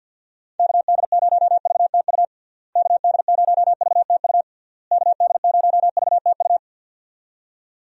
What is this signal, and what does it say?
Morse code, heard as a keyed tone, about 52 wpm, 700 Hz. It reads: KD0VTU KD0VTU KD0VTU
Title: USA Call Signs - 6 Characters - Repeated 3x - 50wpm